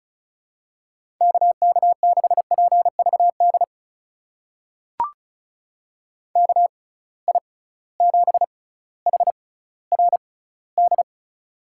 Morse code, heard as a keyed tone, about 36 wpm, spaced as for 18 wpm, 700 Hz. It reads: KK6PVD KI7HRD
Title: USA Call Signs - 6x Character Spacing - 35wpm